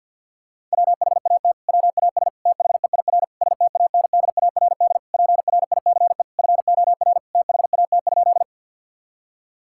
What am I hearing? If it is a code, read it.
Morse code, heard as a keyed tone, about 52 wpm, 700 Hz. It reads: WHAT WAS THEIR STANDARD PRICE FOR THAT?
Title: Sentences from Top 1000 Words - Review Part 6 - 50wpm